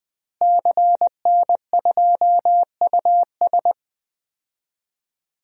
Morse code, heard as a keyed tone, about 20 wpm, 700 Hz. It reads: CN2US
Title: DXCC-100 Call Signs - 5 Characters - Repeated 2x - 20wpm